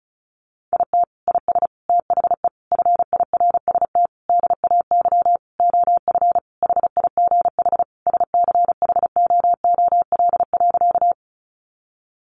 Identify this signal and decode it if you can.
Morse code, heard as a keyed tone, about 34 wpm, 700 Hz. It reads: IT IS THE FIRST DAY OF HIGH SCHOOL.
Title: Additional Easy Phrases 35wpm